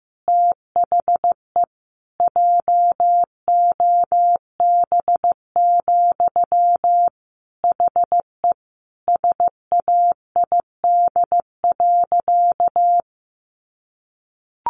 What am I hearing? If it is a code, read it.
Morse code, heard as a keyed tone, about 15 wpm, 700 Hz. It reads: THE JOB, HE SAID.
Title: Sentences from Top 1500 Words 15wpm